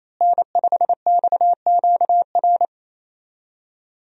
Morse code, heard as a keyed tone, about 28 wpm, 700 Hz. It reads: N5XQR
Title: USA Call Signs - 5 Characters - Repeated 2x - 28wpm